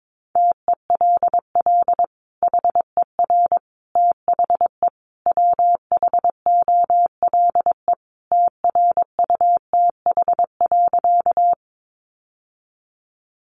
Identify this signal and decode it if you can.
Morse code, heard as a keyed tone, about 22 wpm, 700 Hz. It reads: TELL HER THE WHOLE TRUTH.
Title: Sentences from Top 800 Words 22wpm